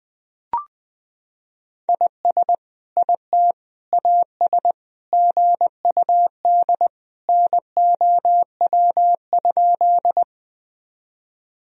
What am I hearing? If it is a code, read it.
Morse code, heard as a keyed tone, about 21 wpm, 700 Hz. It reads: IS IT AS GUD NOW?
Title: Sentences from Top 200 Words and Abbreviations 20wpm